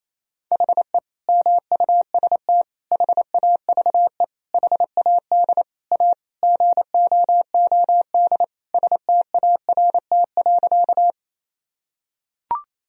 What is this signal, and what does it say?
Morse code, heard as a keyed tone, about 27 wpm, 700 Hz. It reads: HE MUST HAVE HAD A GOOD START.
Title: Sentences from Top 1000 Words - Review Part 1 - 28wpm